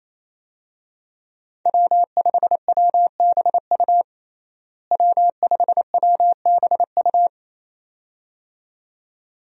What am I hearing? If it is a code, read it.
Morse code, heard as a keyed tone, about 27 wpm, 700 Hz. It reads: W5WBU W5WBU
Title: USA Call Signs - 5 Characters - Repeated 2x - 28wpm